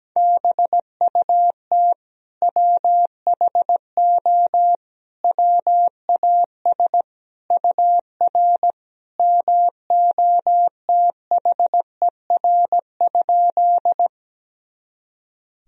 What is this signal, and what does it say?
Morse code, heard as a keyed tone, about 17 wpm, 700 Hz. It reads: BUT WHO WAS UR MOTHER?
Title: Sentences from Top 200 Words and Abbreviations 17wpm